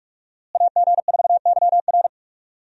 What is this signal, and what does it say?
Morse code, heard as a keyed tone, about 42 wpm, 700 Hz. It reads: AG4YR
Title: USA Call Signs - 5 Characters - Repeated 2x - 45wpm